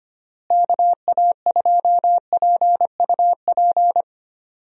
Morse code, heard as a keyed tone, about 25 wpm, 700 Hz. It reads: KA2PUP
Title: USA Call Signs - 6 Characters - Repeated 3x - 25wpm